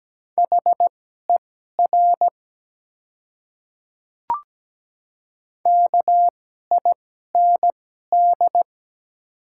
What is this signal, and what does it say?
Morse code, heard as a keyed tone, about 17 wpm, spaced as for 12 wpm, 700 Hz. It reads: HER KIND
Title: 2x Character Spacing - Top 200 Words - 17wpm